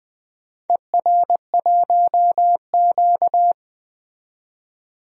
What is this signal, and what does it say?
Morse code, heard as a keyed tone, about 20 wpm, 700 Hz. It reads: ER1Q
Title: DXCC-100 Call Signs - 4 Characters - 20wpm